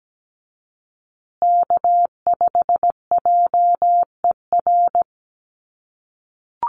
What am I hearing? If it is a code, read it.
Morse code, heard as a keyed tone, about 17 wpm, 700 Hz. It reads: K5JER
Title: USA Call Signs Repeated 3x 17wpm